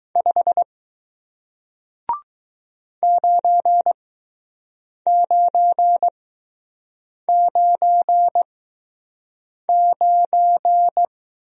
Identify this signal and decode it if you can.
Morse code, heard as a keyed tone, about 20 wpm, spaced as for 6 wpm, 700 Hz. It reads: 5 9999